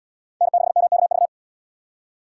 Morse code, heard as a keyed tone, about 74 wpm, 700 Hz. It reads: N6MKV